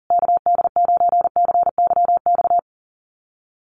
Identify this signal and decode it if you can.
Morse code, heard as a keyed tone, about 40 wpm, 700 Hz. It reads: KD9CYX
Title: USA Call Signs Repeated 3x 40wpm